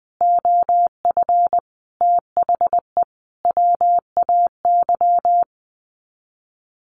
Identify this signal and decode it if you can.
Morse code, heard as a keyed tone, about 20 wpm, 700 Hz. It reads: OF THE WAY
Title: Sets of 5 Words Encore2 20wpm